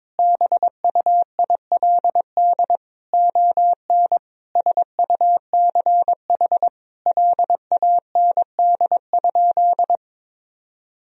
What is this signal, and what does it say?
Morse code, heard as a keyed tone, about 22 wpm, 700 Hz. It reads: BUILD ON SUCH LAND?
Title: Sentences from Top 1000 Words - Review Part 1 - 22wpm